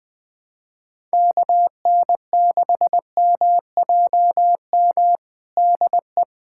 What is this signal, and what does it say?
Morse code, heard as a keyed tone, about 20 wpm, 700 Hz. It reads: KN6MJM DE
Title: USA Callsign DE Callsign - 6 Characters - 20wpm